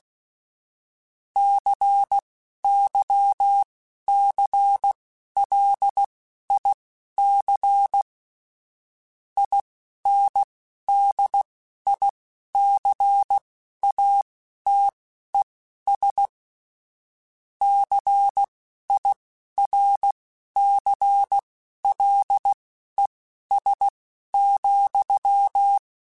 Morse code, heard as a keyed tone, about 16 wpm, spaced as for 11 wpm, 800 Hz. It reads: CYCLIC INDICATES CIRCLES,
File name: Technical Subjects / Technical Subjects-mp3s / XL @10